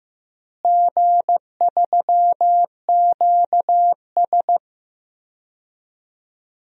Morse code, heard as a keyed tone, about 15 wpm, 700 Hz. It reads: G3QS
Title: DXCC-100 Call Signs - 4 Characters - Repeated 2x - 15wpm